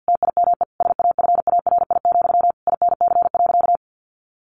Morse code, heard as a keyed tone, about 50 wpm, 700 Hz. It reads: TIME SAFARI, INC.
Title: A Sound Of Thunder 50wpm